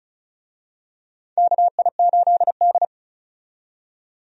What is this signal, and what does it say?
Morse code, heard as a keyed tone, about 36 wpm, 700 Hz. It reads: KI8D
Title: USA Call Signs - 4 Characters - 35wpm